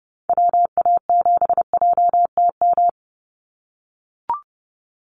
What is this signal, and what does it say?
Morse code, heard as a keyed tone, about 30 wpm, 700 Hz. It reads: WA7JTM
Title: NA SOTA Activator Callsigns 30wpm